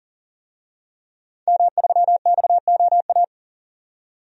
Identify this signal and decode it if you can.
Morse code, heard as a keyed tone, about 40 wpm, 700 Hz. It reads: M3XOA
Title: DXCC-100 Call Signs - 5 Characters - 40wpm